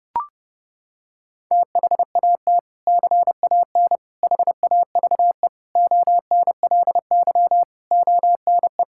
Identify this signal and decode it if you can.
Morse code, heard as a keyed tone, about 29 wpm, 700 Hz. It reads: THAT CAN HAVE ONLY ONE
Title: Sentences from Top 1000 Words - Review Part 6 - 30wpm